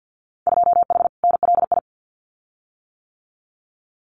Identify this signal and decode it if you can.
Morse code, heard as a keyed tone, about 50 wpm, 700 Hz. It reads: 2H NLI
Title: Winter Field Day Exchanges Encore 50wpm